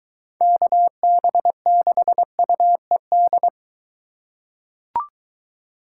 Morse code, heard as a keyed tone, about 22 wpm, 700 Hz. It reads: KB6UED